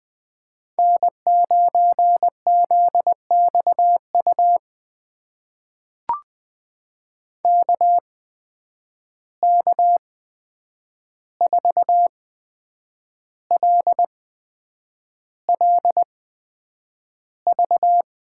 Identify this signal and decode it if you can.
Morse code, heard as a keyed tone, about 20 wpm, spaced as for 8 wpm, 700 Hz. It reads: N9ZXU KK4LLV
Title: USA Call Signs - 8x Character Spacing - 20wpm